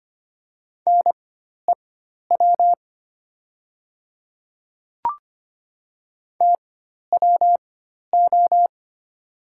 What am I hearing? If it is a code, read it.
Morse code, heard as a keyed tone, about 25 wpm, spaced as for 12 wpm, 700 Hz. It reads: NEW TWO
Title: 4x Character Spacing - Top 300 Words - 25wpm